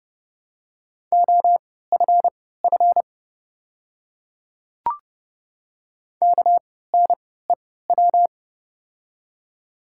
Morse code, heard as a keyed tone, about 30 wpm, spaced as for 17 wpm, 700 Hz. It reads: OFF KNEW
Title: 3x Character Spacing - Top 500 Words - 30wpm